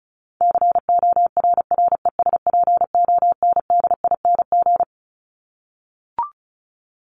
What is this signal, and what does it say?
Morse code, heard as a keyed tone, about 34 wpm, 700 Hz. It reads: CORRESPONDING
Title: ING Suffix - Top 100 Words with Base Words 35wpm